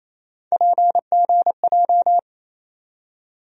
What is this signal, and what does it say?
Morse code, heard as a keyed tone, about 27 wpm, 700 Hz. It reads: PGJ
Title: USA Call Sign Postfixes - 3 Letters 28wpm